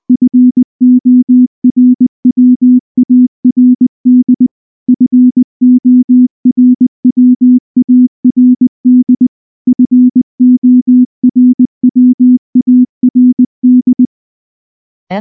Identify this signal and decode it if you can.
Morse code, heard as a keyed tone, about 19 wpm, 250 Hz. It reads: FORWARD FORWARD FORWARD
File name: morse / morse-training-20WPM-4